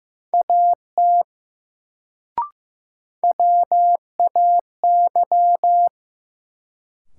Morse code, heard as a keyed tone, about 15 wpm, 700 Hz. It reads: AT WAY
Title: Sentences from Top 100 Words - New Word Drill - 15wpm